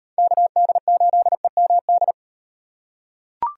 Morse code, heard as a keyed tone, about 38 wpm, 700 Hz. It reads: KD8EMD